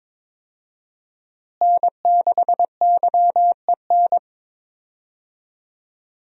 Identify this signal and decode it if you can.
Morse code, heard as a keyed tone, about 22 wpm, 700 Hz. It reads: N6YEN
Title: USA Call Signs - 5 Characters - Repeated 3x - 22wpm